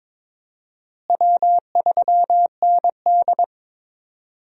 Morse code, heard as a keyed tone, about 22 wpm, 700 Hz. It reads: W3ND
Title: USA Call Signs - 4 Characters - Repeated 3x - 22wpm